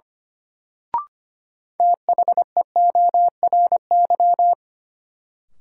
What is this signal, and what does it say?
Morse code, heard as a keyed tone, about 25 wpm, 700 Hz. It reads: THEORY
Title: Sentences from Top 600 Words - New Word Drill - 25wpm